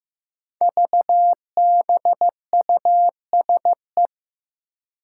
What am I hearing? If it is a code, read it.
Morse code, heard as a keyed tone, about 15 wpm, 700 Hz. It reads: VBUSE
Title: Five Challenging Letter-Number Combinations 15wpm